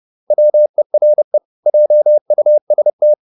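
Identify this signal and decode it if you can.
Morse code, heard as a keyed tone, about 30 wpm, 600 Hz. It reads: WERE JUST
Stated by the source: Morse code at 30 wpm, the novel Never by Ken Follett in English, from CW med Gnister.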